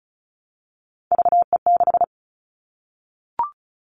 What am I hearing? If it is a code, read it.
Morse code, heard as a keyed tone, about 34 wpm, 700 Hz. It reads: VE6
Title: Callsign Prefixes for Top 50 DXCC - 35wpm